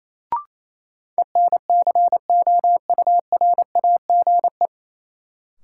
Morse code, heard as a keyed tone, about 28 wpm, 700 Hz. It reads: ENCOURAGE
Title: Sentences from Top 1400 Words - New Word Drill Alt - 28wpm